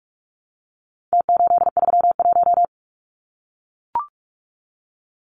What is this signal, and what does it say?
Morse code, heard as a keyed tone, about 44 wpm, 700 Hz. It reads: T831
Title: Numbers - Contest Style - 4-Digit 45wpm